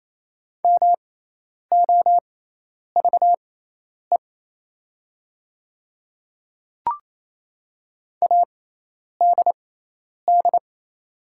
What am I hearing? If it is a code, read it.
Morse code, heard as a keyed tone, about 27 wpm, spaced as for 10 wpm, 700 Hz. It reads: MOVE ADD
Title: 6x Character Spacing - Top 200 Words - 28wpm